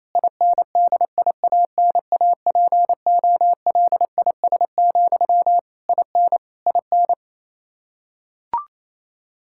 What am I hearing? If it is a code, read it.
Morse code, heard as a keyed tone, about 28 wpm, 700 Hz. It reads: INDIANAPOLIS, IN IN
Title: Practice QSO - Exchange 1 - Beginning - 28wpm